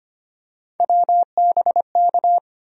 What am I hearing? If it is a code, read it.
Morse code, heard as a keyed tone, about 25 wpm, 700 Hz. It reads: WBK
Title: USA Call Sign Postfixes - 3 Letters 25wpm